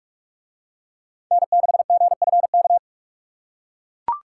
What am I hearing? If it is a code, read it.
Morse code, heard as a keyed tone, about 41 wpm, 700 Hz. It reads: N6GFK